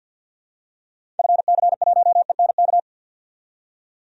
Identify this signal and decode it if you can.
Morse code, heard as a keyed tone, about 51 wpm, 700 Hz. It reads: FC1ENK
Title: DXCC-100 Call Signs - 6 Characters - 50wpm